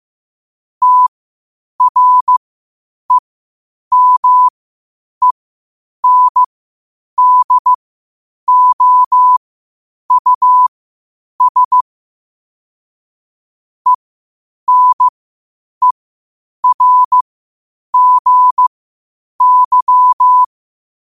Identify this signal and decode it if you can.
Morse code, heard as a keyed tone, about 15 wpm, spaced as for 8 wpm, 1000 Hz. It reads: TREMENDOUS ENERGY
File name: qotdfiles / 8wpm